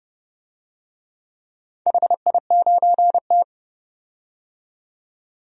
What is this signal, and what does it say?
Morse code, heard as a keyed tone, about 30 wpm, 700 Hz. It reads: HI9T
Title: DXCC-100 Call Signs - 4 Characters - 30wpm